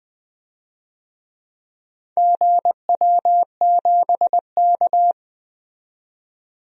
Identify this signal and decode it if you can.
Morse code, heard as a keyed tone, about 20 wpm, 700 Hz. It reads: GW7K
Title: DXCC-100 Call Signs - 4 Characters - 20wpm